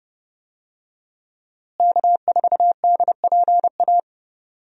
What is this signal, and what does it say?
Morse code, heard as a keyed tone, about 32 wpm, 700 Hz. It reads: K4DPA
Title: USA Call Signs - 5 Characters - Repeated 2x - 30wpm